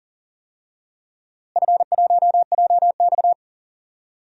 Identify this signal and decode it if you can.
Morse code, heard as a keyed tone, about 40 wpm, 700 Hz. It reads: F1JX
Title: DXCC-100 Call Signs - 4 Characters - Repeated 2x - 40wpm